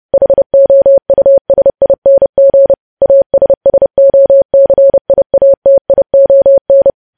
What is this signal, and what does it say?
Morse code, heard as a keyed tone, about 30 wpm, 550 Hz. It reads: HOUSING ASSOCIATION
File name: podcast / News headlines-1x-30x30